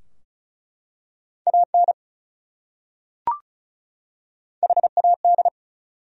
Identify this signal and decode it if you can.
Morse code, heard as a keyed tone, about 36 wpm, 700 Hz. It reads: AN HAD
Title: Top 100 Words 35wpm